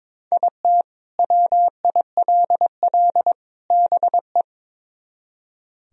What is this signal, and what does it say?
Morse code, heard as a keyed tone, about 24 wpm, 700 Hz. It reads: IT WILL BE
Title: Speed-Racing - Sets of 3 Words 22wpm